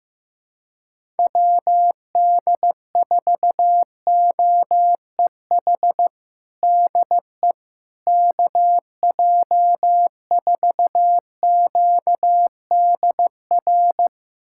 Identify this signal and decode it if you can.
Morse code, heard as a keyed tone, about 16 wpm, 700 Hz. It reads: WD4OEH DE KJ4QDR
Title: USA Callsign DE Callsign - 6 Characters - 15wpm